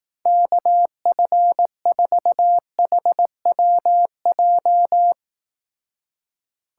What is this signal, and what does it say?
Morse code, heard as a keyed tone, about 18 wpm, 700 Hz. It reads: KF4HWJ
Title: Speed Racing - USA 2x3 Call Signs 15wpm